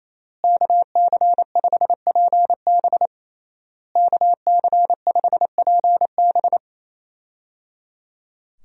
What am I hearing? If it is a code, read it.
Morse code, heard as a keyed tone, about 28 wpm, 700 Hz. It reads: KC5PB KC5PB
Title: USA Call Signs - 5 Characters - Repeated 3x - 28wpm